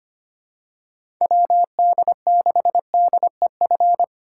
Morse code, heard as a keyed tone, about 25 wpm, 700 Hz. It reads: WD6DEF
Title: USA Call Signs - 6 Characters - 4x Character Spacing - 25wpm